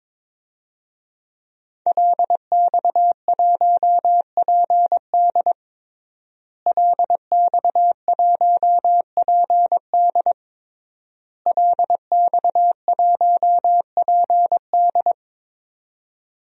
Morse code, heard as a keyed tone, about 22 wpm, 700 Hz. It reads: LX1PD LX1PD LX1PD
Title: DXCC-100 Call Signs - 5 Characters - Repeated 3x - 22wpm